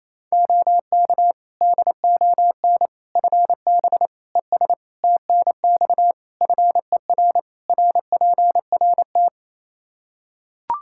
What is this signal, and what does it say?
Morse code, heard as a keyed tone, about 28 wpm, 700 Hz. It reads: OK DON FB ES TNX FER RPRT
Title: Practice QSO - Exchange 2 - Info - 28wpm